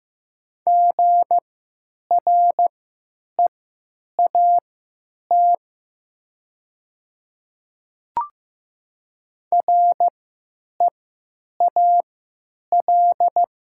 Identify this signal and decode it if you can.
Morse code, heard as a keyed tone, about 15 wpm, spaced as for 9 wpm, 700 Hz. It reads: GREAT REAL
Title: 3x Character Spacing - Top 500 Words - 15wpm